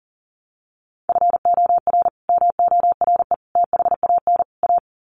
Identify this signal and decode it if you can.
Morse code, heard as a keyed tone, about 38 wpm, 700 Hz. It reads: FOR MORE THAN A
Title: Sets of 4 Words 40wpm